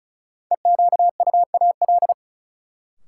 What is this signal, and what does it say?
Morse code, heard as a keyed tone, about 36 wpm, 700 Hz. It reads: EQUAL
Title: Sentences from Top 1400 Words - New Word Drill - 35wpm